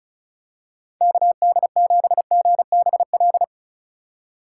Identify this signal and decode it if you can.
Morse code, heard as a keyed tone, about 34 wpm, 700 Hz. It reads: KD7GBL
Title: USA Call Signs - 6 Characters - 4x Character Spacing - 35wpm